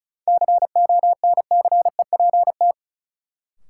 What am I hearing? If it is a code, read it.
Morse code, heard as a keyed tone, about 34 wpm, 700 Hz. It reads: CONCEPT
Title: Sentences from Top 1100 Words - New Word Drill Alt - 35wpm